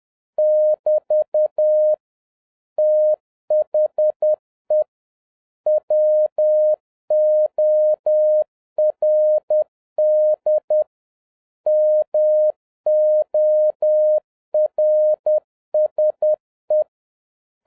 Morse code, heard as a keyed tone, about 10 wpm, 610 Hz. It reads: <BT> THE WORD MORSE